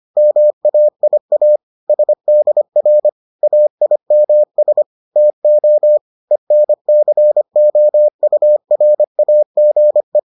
Morse code, heard as a keyed tone, about 25 wpm, 600 Hz. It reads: MAIA SDR AIMS TO ENCOURAGE